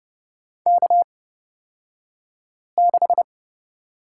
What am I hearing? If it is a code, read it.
Morse code, heard as a keyed tone, about 31 wpm, spaced as for 5 wpm, 690 Hz. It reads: K6
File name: Lesson 36d - BT - Two Characters 30wpm